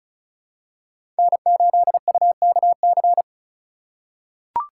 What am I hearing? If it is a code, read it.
Morse code, heard as a keyed tone, about 36 wpm, 700 Hz. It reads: N8UKC
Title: USA Call Signs - 5 Characters - 2x Character Spacing - 35wpm